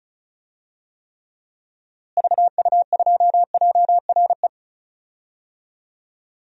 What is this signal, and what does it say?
Morse code, heard as a keyed tone, about 34 wpm, 700 Hz. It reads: VU2JRE
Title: DXCC-100 Call Signs - 6 Characters - Repeated 3x - 35wpm